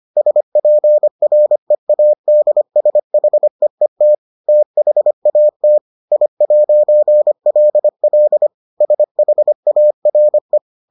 Morse code, heard as a keyed tone, about 25 wpm, 600 Hz. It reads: SPREADSHEET THAT I'LL SHARE